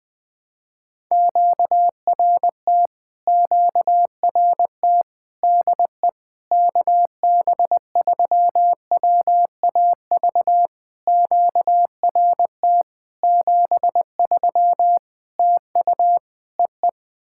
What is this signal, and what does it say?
Morse code, heard as a keyed tone, about 20 wpm, 690 Hz. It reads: QRT QRT DE KB3WAV QRT 73 TU EE
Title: POTA QSO Practice - 20wpm